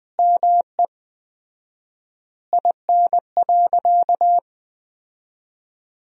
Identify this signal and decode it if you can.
Morse code, heard as a keyed tone, about 20 wpm, 700 Hz. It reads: ME IN.
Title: 4x Word-Spacing - Sentences from Top 300 Words - 20wpm